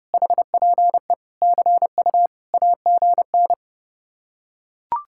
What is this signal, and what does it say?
Morse code, heard as a keyed tone, about 30 wpm, 700 Hz. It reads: HPE CU AGN